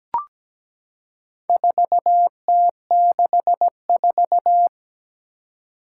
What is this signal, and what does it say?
Morse code, heard as a keyed tone, about 17 wpm, 700 Hz. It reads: 4T64
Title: Four Challenging Letter-Number Combinations 17wpm